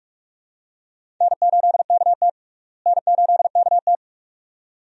Morse code, heard as a keyed tone, about 46 wpm, 700 Hz. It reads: N8KT N8KT
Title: USA Call Signs - 4 Characters - Repeated 2x - 45wpm